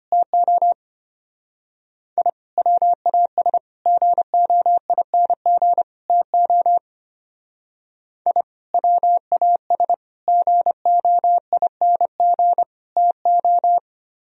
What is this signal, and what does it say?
Morse code, heard as a keyed tone, about 31 wpm, 700 Hz. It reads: TO I WAS GOING TO I WAS GOING TO
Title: Speed-Racing - Sets of 4 Words 25wpm